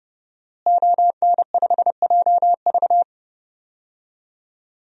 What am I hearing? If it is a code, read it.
Morse code, heard as a keyed tone, about 30 wpm, 700 Hz. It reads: ON5JV
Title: DXCC-100 Call Signs - 5 Characters - Repeated 2x - 30wpm